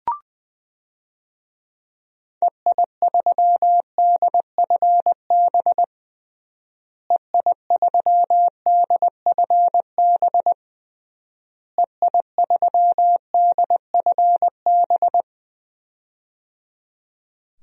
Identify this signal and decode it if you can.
Morse code, heard as a keyed tone, about 20 wpm, 700 Hz. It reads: EI3DFB EI3DFB EI3DFB
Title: DXCC-100 Call Signs - 6 Characters - Repeated 3x - 20wpm